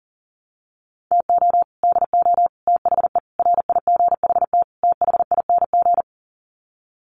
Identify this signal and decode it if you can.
Morse code, heard as a keyed tone, about 40 wpm, 700 Hz. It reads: TO DO THE RIGHT THING
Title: Sets of 5 Words 40wpm